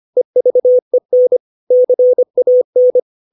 Morse code, heard as a keyed tone, about 25 wpm, 500 Hz. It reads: EVEN CAN
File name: 632 words / 632 Words 25WPM